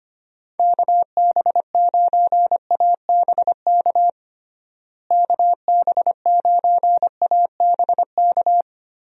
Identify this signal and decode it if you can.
Morse code, heard as a keyed tone, about 25 wpm, 700 Hz. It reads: KB9ABK KB9ABK
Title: USA Call Signs - 6 Characters - Repeated 3x - 25wpm